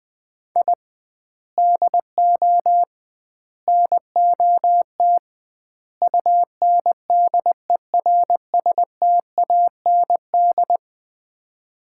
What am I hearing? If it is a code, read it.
Morse code, heard as a keyed tone, about 20 wpm, 700 Hz. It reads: I DO NOT UNDERSTAND
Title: 2x Word-Spacing - Sets of 4 Words - 20wpm